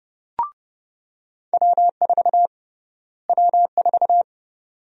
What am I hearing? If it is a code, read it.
Morse code, heard as a keyed tone, about 30 wpm, 700 Hz. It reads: W4 W4
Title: USA Call Sign Prefixes - Repeated 2x - 30wpm